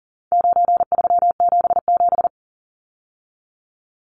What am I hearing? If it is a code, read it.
Morse code, heard as a keyed tone, about 40 wpm, 700 Hz. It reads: 9377
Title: Numbers - 4-Digit 40wpm